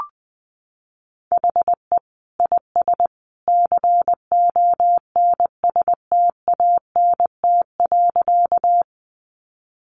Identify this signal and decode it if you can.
Morse code, heard as a keyed tone, about 20 wpm, 700 Hz. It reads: HE IS CONSTANT.
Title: Sentences from Top 1500 Words 20wpm